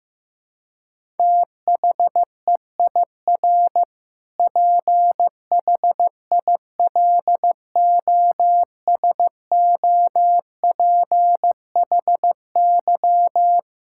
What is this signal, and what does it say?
Morse code, heard as a keyed tone, about 15 wpm, 700 Hz. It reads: THEIR PHILOSOPHY
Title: Sentences from Top 1000 Words - Review Part 6 - 15wpm